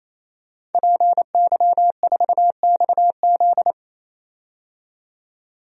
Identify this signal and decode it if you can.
Morse code, heard as a keyed tone, about 28 wpm, 700 Hz. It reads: PY4XZ
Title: DXCC-100 Call Signs - 5 Characters - 28wpm